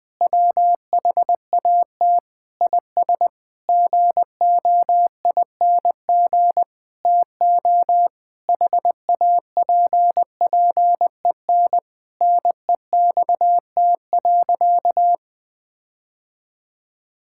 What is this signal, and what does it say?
Morse code, heard as a keyed tone, about 20 wpm, 700 Hz. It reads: WHAT IS GOING TO HAPPEN NEXT.
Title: Sentences with 7-Word Phrases - 20wpm